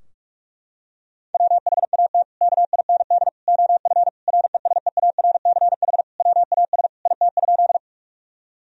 Morse code, heard as a keyed tone, about 46 wpm, 700 Hz. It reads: WHAT KIND OF RESEARCH WAS IT?
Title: Sentences from Top 1000 Words - Review Part 4 - 45wpm